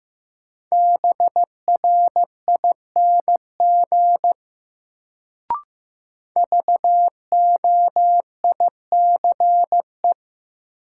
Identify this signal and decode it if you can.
Morse code, heard as a keyed tone, about 15 wpm, 700 Hz. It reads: BRING VOICE
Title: Sentences from Top 500 Words - New Word Drill - 15wpm